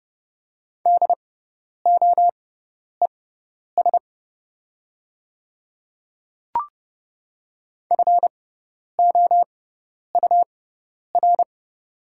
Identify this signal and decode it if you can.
Morse code, heard as a keyed tone, about 30 wpm, spaced as for 10 wpm, 700 Hz. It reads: DOES FOUR
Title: 6x Character Spacing - Top 300 Words - 30wpm